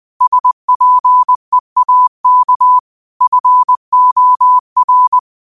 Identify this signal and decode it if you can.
Morse code, heard as a keyed tone, about 20 wpm, 1000 Hz. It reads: SPEAK FOR
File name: qotdfiles / 20wpm